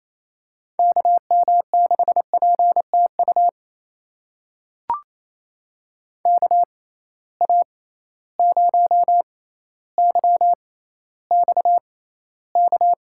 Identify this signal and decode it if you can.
Morse code, heard as a keyed tone, about 28 wpm, spaced as for 14 wpm, 700 Hz. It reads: KM6PTU KA0YXK
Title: USA Call Signs - 6 Characters - 6x Character Spacing - 28wpm